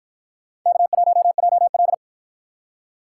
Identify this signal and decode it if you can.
Morse code, heard as a keyed tone, about 53 wpm, 700 Hz. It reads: K1JL